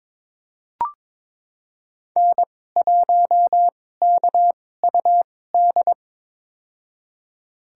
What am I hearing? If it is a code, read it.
Morse code, heard as a keyed tone, about 22 wpm, spaced as for 16 wpm, 700 Hz. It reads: N1KUD
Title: USA Call Signs - 5 Characters - 2x Character Spacing - 22wpm